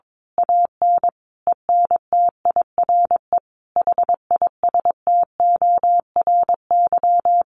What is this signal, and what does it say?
Morse code, heard as a keyed tone, about 22 wpm, 700 Hz. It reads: AN ENTIRE HISTORY
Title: A Sound Of Thunder 22wpm